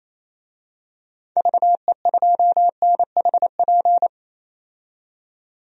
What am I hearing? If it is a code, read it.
Morse code, heard as a keyed tone, about 28 wpm, 700 Hz. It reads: VE2NHP
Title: DXCC-100 Call Signs - 6 Characters - 28wpm